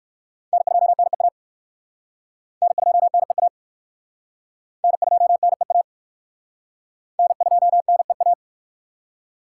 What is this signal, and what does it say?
Morse code, heard as a keyed tone, about 64 wpm, 700 Hz. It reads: N2NEA N2NEA N2NEA N2NEA